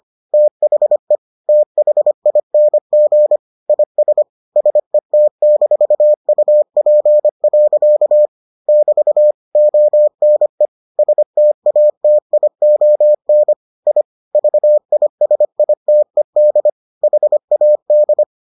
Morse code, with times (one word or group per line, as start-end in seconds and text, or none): 0.3-1.2 THE
1.5-3.4 THING
3.7-4.2 IS
4.6-8.3 SET-UP.
8.7-9.3 <BT>
9.6-10.7 ONE
11.0-13.5 STATION
13.9-14.0 I
14.3-16.7 VISITED
17.0-18.2 HAD